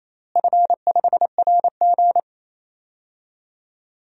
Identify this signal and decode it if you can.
Morse code, heard as a keyed tone, about 27 wpm, 700 Hz. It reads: F5RG
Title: DXCC-100 Call Signs - 4 Characters - Repeated 3x - 28wpm